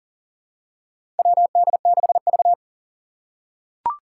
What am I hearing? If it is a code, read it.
Morse code, heard as a keyed tone, about 40 wpm, 700 Hz. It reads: WD6V
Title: Four Challenging Letter-Number Combinations 40wpm